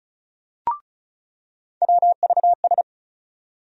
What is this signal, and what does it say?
Morse code, heard as a keyed tone, about 36 wpm, 700 Hz. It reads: WVS
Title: USA Call Sign Postfixes - 3 Letters 35wpm